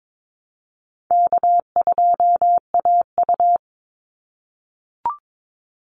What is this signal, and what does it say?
Morse code, heard as a keyed tone, about 22 wpm, 700 Hz. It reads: K2AU
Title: Call Signs - 22wpm